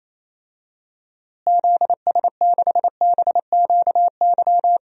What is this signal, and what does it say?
Morse code, heard as a keyed tone, about 28 wpm, 700 Hz. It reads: ZS6BQY
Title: DXCC-100 Call Signs - 6 Characters - Repeated 3x - 28wpm